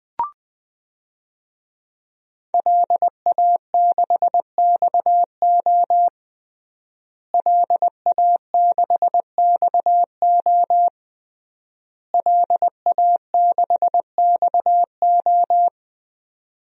Morse code, heard as a keyed tone, about 21 wpm, 700 Hz. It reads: LA6XO LA6XO LA6XO
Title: DXCC-100 Call Signs - 5 Characters - Repeated 3x - 20wpm